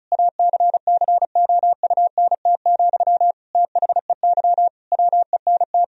Morse code, heard as a keyed tone, about 34 wpm, 700 Hz. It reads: ACCOUNT, THEY WENT
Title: Sentences from Top 1000 Words - Review Part 4 - 35wpm